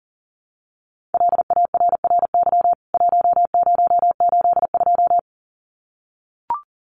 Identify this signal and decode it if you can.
Morse code, heard as a keyed tone, about 40 wpm, 700 Hz. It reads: LARRY 1082
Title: CWT Exchange 40wpm